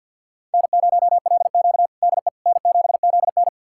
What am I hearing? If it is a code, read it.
Morse code, heard as a keyed tone, about 50 wpm, 700 Hz. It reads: N0LQ DE N7ZN
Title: USA Callsign DE Callsign - 4 Characters - 50wpm